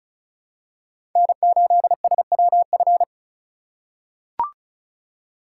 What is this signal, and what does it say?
Morse code, heard as a keyed tone, about 36 wpm, 700 Hz. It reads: N8SWF
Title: USA Call Signs - 2x Character Spacing - 35wpm